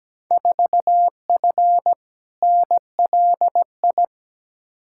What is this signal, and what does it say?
Morse code, heard as a keyed tone, about 17 wpm, 700 Hz. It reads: 4F NLI
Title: Field Day Exchanges Encore - 17wpm